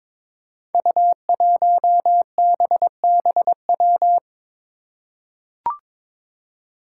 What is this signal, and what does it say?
Morse code, heard as a keyed tone, about 23 wpm, 700 Hz. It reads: U1BBW
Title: Five Challenging Letter-Number Combinations 22wpm